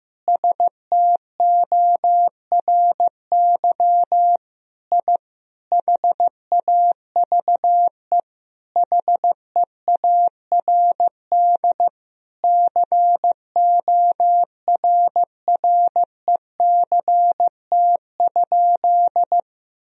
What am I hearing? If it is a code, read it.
Morse code, heard as a keyed tone, about 15 wpm, 700 Hz. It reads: STORY I HAVE HEARD CORRECT?
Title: Sentences from Top 1000 Words - Review Part 2 - 15wpm